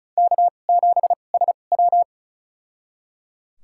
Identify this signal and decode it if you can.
Morse code, heard as a keyed tone, about 36 wpm, spaced as for 25 wpm, 700 Hz. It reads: K7SW
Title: USA Call Signs - 4 Characters - 2x Character Spacing - 35wpm